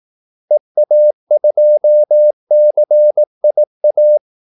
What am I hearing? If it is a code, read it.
Morse code, heard as a keyed tone, about 18 wpm, 600 Hz. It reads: EA2CIA
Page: Kaldesignaler 18 wpm